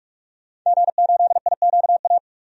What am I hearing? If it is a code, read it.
Morse code, heard as a keyed tone, about 43 wpm, 700 Hz. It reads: G8IQA
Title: DXCC-100 Call Signs - 5 Characters - 45wpm